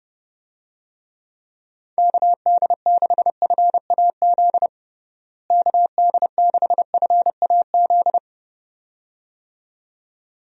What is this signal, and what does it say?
Morse code, heard as a keyed tone, about 30 wpm, 700 Hz. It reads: KD6FAZ KD6FAZ
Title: USA Call Signs - 6 Characters - Repeated 2x - 30wpm